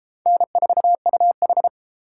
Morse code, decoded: N4UH

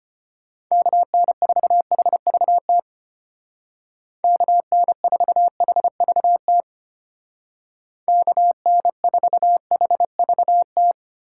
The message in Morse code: KN4HVT KN4HVT KN4HVT

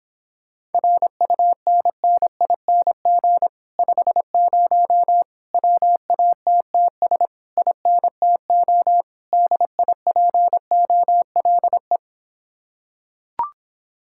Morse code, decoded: RUNNING 50 WATTS INTO DIPOLE

